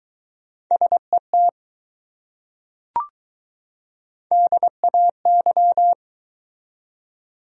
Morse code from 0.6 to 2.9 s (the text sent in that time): SET